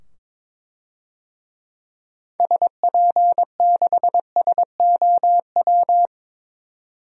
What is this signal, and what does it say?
Morse code, heard as a keyed tone, about 22 wpm, 700 Hz. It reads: SP6SOW
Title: DXCC-100 Call Signs - 6 Characters - 22wpm